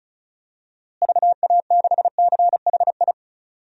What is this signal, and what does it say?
Morse code, heard as a keyed tone, about 34 wpm, 700 Hz. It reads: VA6CHI
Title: DXCC-100 Call Signs - 6 Characters - Repeated 2x - 35wpm